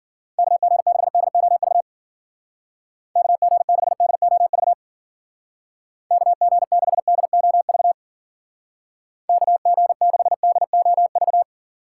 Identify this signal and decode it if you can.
Morse code, heard as a keyed tone, about 51 wpm, 700 Hz. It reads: KG6DOV KG6DOV KG6DOV KG6DOV